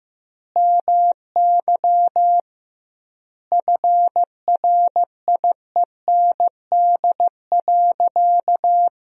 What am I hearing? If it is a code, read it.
Morse code, heard as a keyed tone, about 15 wpm, 700 Hz. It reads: MY FRIEND.
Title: 2x Word-Spacing - Sentences from Top 300 Words - 15wpm